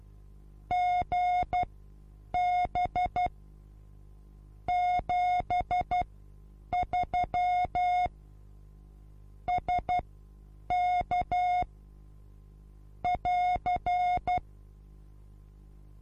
Morse code, heard as a keyed tone, about 12 wpm, spaced as for 8 wpm, 720 Hz. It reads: GB 73 SK <AR>
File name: Onlinelehrgang / morsen / DJ4UF Morsen23